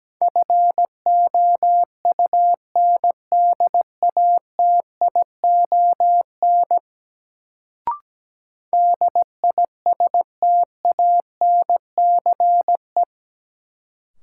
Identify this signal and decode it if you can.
Morse code, heard as a keyed tone, about 17 wpm, 700 Hz. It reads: FOUNDATION DISTANCE
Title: Sentences from Top 1200 Words - New Word Drill - 17wpm